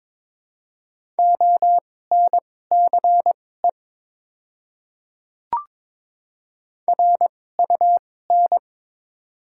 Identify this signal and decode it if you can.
Morse code, heard as a keyed tone, about 22 wpm, spaced as for 16 wpm, 700 Hz. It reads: ONCE RUN
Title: 2x Character Spacing - Top 300 Words - 22wpm